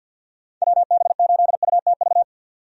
Morse code, heard as a keyed tone, about 46 wpm, 700 Hz. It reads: WB8UTV
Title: USA Call Signs - 6 Characters - 50wpm